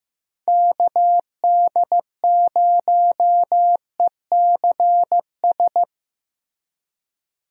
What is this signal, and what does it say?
Morse code, heard as a keyed tone, about 15 wpm, 700 Hz. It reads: KD0ECS
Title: USA Call Signs - 6 Characters - Repeated 3x - 15wpm